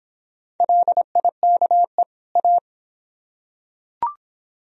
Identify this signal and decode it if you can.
Morse code, heard as a keyed tone, about 22 wpm, 700 Hz. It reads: LIKE A